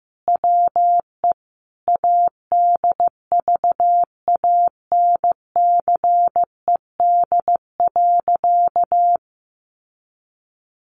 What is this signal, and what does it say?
Morse code, heard as a keyed tone, about 15 wpm, 700 Hz. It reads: WE ADVANCED.
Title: Sentences from Top 1200 Words 15wpm